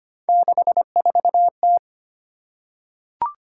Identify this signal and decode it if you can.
Morse code, heard as a keyed tone, about 25 wpm, 700 Hz. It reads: 64T
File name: Three Letter-Number Combinations - 25wpm